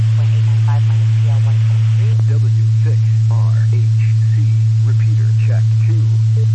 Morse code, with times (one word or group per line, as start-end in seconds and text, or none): none